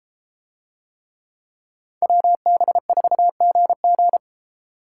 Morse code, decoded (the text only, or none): WB4GG